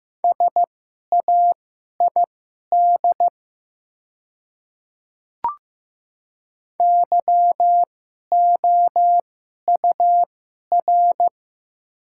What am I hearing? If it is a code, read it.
Morse code, heard as a keyed tone, about 15 wpm, spaced as for 11 wpm, 700 Hz. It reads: SAID YOUR
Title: 2x Character Spacing - Top 100 Words - 15wpm